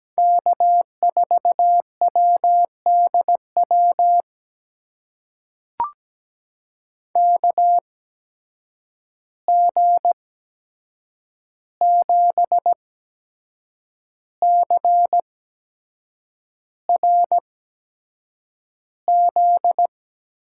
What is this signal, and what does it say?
Morse code, heard as a keyed tone, about 17 wpm, spaced as for 7 wpm, 700 Hz. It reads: K4WDW KG7CRZ
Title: USA Call Signs - 8x Character Spacing - 17wpm